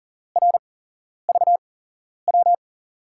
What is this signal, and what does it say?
Morse code, heard as a keyed tone, about 39 wpm, spaced as for 11 wpm, 700 Hz. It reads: RVW
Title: USA Call Signs - 8x Character Spacing - 40wpm